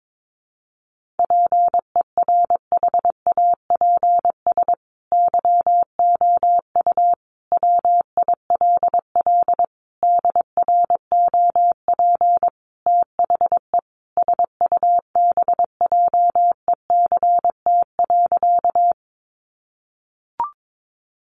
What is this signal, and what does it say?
Morse code, heard as a keyed tone, about 22 wpm, 700 Hz. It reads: PERHAPS YOU WILL DROP THE SUBJECT.
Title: Sentences from Top 1400 Words 22wpm